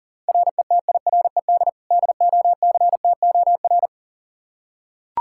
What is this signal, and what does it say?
Morse code, heard as a keyed tone, about 40 wpm, 700 Hz. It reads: RETIRED DOCTOR
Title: Practice QSO - Exchange 3 - More Info - 40wpm